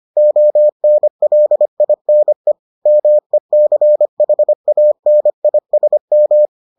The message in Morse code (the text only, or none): ONLINE MECHANISM